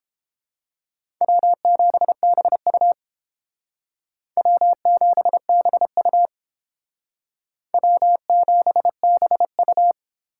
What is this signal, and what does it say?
Morse code, heard as a keyed tone, about 29 wpm, 700 Hz. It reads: W7BU W7BU W7BU